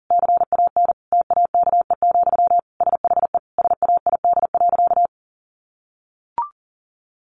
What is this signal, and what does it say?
Morse code, heard as a keyed tone, about 40 wpm, 700 Hz. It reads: CAN TAKE, SHE SAID.
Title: Sentences from Top 1500 Words 40wpm